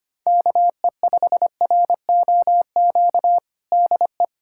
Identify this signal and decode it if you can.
Morse code, heard as a keyed tone, about 25 wpm, 700 Hz. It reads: KE5ROQ DE
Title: USA Callsign DE Callsign - 6 Characters - 25wpm